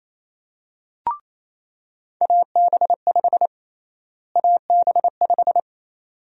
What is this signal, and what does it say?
Morse code, heard as a keyed tone, about 28 wpm, 700 Hz. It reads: AB5 AB5
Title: USA Call Sign Prefixes - Repeated 2x - 28wpm